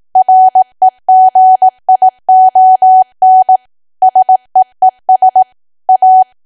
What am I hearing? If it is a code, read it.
Morse code, heard as a keyed tone, about 18 wpm, 750 Hz. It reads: REGION SEES A
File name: cwa / W1AW-Level2 / 18WPM